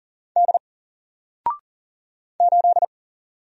Morse code, heard as a keyed tone, about 40 wpm, 700 Hz. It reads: D 8
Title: Single Character (Letter:Number:Punctuation:Prosign) 40wpm